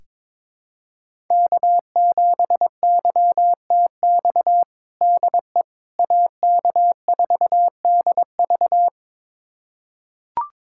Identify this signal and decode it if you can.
Morse code, heard as a keyed tone, about 22 wpm, 700 Hz. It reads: K7YTX DE AK4DV
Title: USA Callsign DE Callsign - 5 Characters - 22wpm